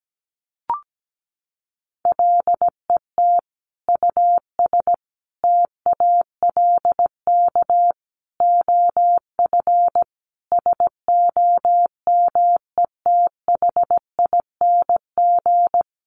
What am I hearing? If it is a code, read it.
Morse code, heard as a keyed tone, about 17 wpm, 700 Hz. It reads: LET US TALK OF SOMETHING
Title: Sentences from Top 800 Words 17wpm